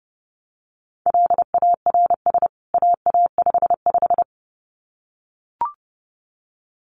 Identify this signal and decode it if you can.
Morse code, heard as a keyed tone, about 30 wpm, 700 Hz. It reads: LARS AA55
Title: CWT Exchange 30wpm